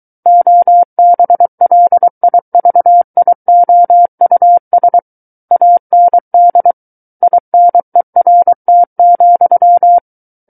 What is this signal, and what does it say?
Morse code, heard as a keyed tone, about 23 wpm, 700 Hz. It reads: OBLIVIOUS AND INERT,
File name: Book One / 14-Chapter 14 - 23 WPM